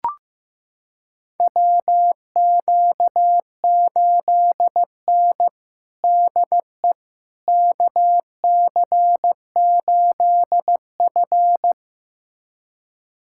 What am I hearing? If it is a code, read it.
Morse code, heard as a keyed tone, about 15 wpm, 700 Hz. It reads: WQ8N DE KC8F
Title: USA Callsign DE Callsign - 4 Characters - 15wpm